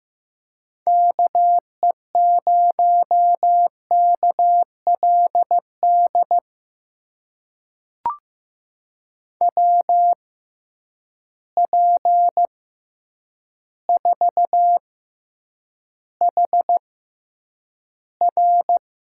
Morse code, heard as a keyed tone, about 15 wpm, spaced as for 8 wpm, 700 Hz. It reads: KE0KLD WP4HR
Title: USA Call Signs - 6 Characters - 6x Character Spacing - 15wpm